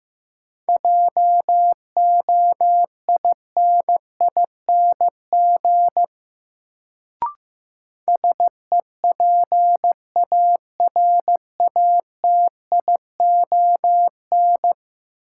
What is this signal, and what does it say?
Morse code, heard as a keyed tone, about 15 wpm, 700 Hz. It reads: JOINING SEPARATION
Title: Sets of 5 Words - New Word Drill - 15wpm